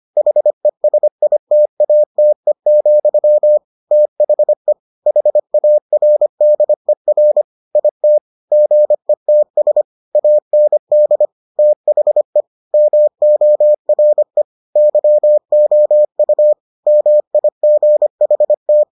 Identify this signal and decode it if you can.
Morse code, heard as a keyed tone, about 25 wpm, 600 Hz. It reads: HESITATE, THE HARDER IT GETS AND THE MORE YOU MIGHT